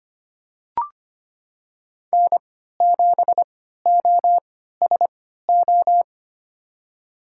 Morse code, heard as a keyed tone, about 25 wpm, spaced as for 14 wpm, 700 Hz. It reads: N7OSO
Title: USA Call Signs - 5 Characters - 3x Character Spacing - 25wpm